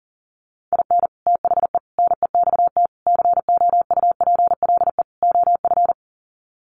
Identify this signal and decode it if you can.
Morse code, heard as a keyed tone, about 40 wpm, 700 Hz. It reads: IN THE NEXT COUPLE OF
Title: Sets of 5 Words Encore2 40wpm